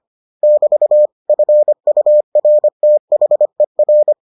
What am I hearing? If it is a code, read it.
Morse code, heard as a keyed tone, about 25 wpm, 600 Hz. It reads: <BT> FURTHER